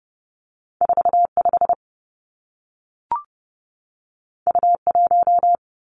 Morse code, heard as a keyed tone, about 30 wpm, 700 Hz. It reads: 4 5 U 1
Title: Two Challenging Letter-Number Combinations 30wpm